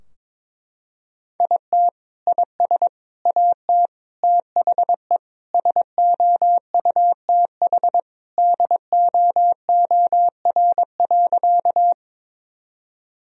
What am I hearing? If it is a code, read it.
Morse code, heard as a keyed tone, about 22 wpm, 700 Hz. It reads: IT IS AT THE SOUTH DOOR.
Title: Sentences from Top 1000 Words - Review Part 2 - 22wpm